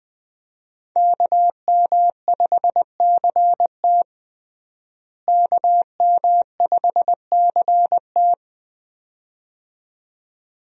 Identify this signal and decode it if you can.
Morse code, heard as a keyed tone, about 20 wpm, 700 Hz. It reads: KM5CT KM5CT
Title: USA Call Signs - 5 Characters - Repeated 2x - 20wpm